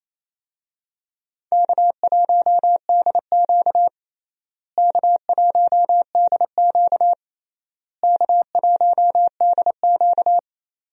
Morse code, heard as a keyed tone, about 28 wpm, 700 Hz. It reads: K1DQ K1DQ K1DQ
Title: USA Call Signs - 4 Characters - Repeated 3x - 28wpm